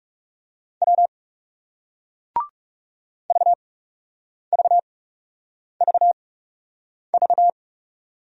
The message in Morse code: W VVVV

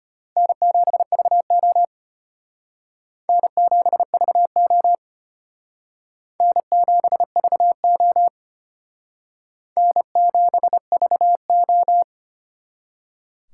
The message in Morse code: N7VO N7VO N7VO N7VO

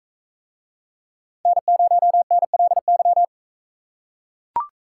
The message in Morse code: N0NLY

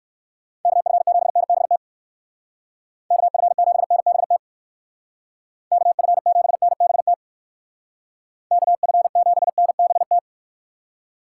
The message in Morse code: KF7NBT KF7NBT KF7NBT KF7NBT

